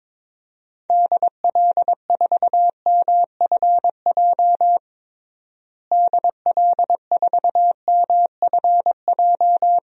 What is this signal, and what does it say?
Morse code, heard as a keyed tone, about 21 wpm, 700 Hz. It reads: DL4MFJ DL4MFJ
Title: DXCC-100 Call Signs - 6 Characters - Repeated 3x - 22wpm